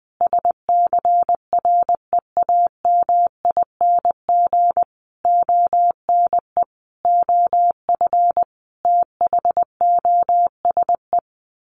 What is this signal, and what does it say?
Morse code, heard as a keyed tone, about 20 wpm, 700 Hz. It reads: SCREAMING ONE OF THOSE
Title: A Sound Of Thunder 20wpm